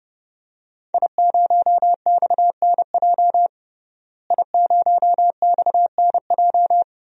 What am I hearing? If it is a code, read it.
Morse code, heard as a keyed tone, about 32 wpm, 700 Hz. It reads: I0XNJ I0XNJ
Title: DXCC-100 Call Signs - 5 Characters - Repeated 2x - 30wpm